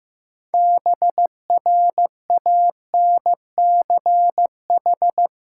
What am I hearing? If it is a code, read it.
Morse code, heard as a keyed tone, about 15 wpm, 700 Hz. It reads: BRANCH
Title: Practice QSO - Exchange 3 - More Info - 15wpm